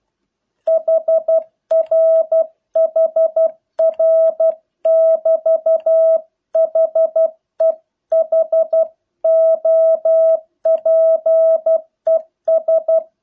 Morse code, heard as a keyed tone, about 12 wpm, 650 Hz. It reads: HR HR <BT> HE HOPES